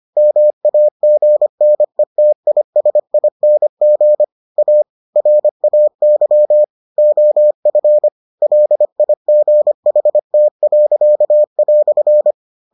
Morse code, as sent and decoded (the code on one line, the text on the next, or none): -- .- --. -. - .. ... .. -. --. / .- / .-. .- -.-- / --- ..-. / .-.. .. --. .... - .-.-.- .-..-.
MAGNETISING A RAY OF LIGHT."